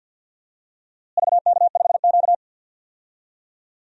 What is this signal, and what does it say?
Morse code, heard as a keyed tone, about 49 wpm, 700 Hz. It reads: VK5Q
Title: DXCC-100 Call Signs - 4 Characters - Repeated 2x - 50wpm